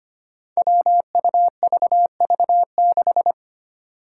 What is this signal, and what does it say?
Morse code, heard as a keyed tone, about 25 wpm, 700 Hz. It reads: WUVV6
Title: Five Challenging Letter-Number Combinations 25wpm